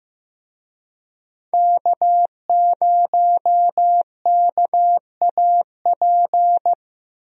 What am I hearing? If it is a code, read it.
Morse code, heard as a keyed tone, about 16 wpm, 700 Hz. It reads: K0KAP
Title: USA Call Signs - 5 Characters - Repeated 3x - 15wpm